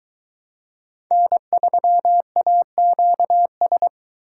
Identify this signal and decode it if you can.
Morse code, heard as a keyed tone, about 23 wpm, 700 Hz. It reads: N3AQS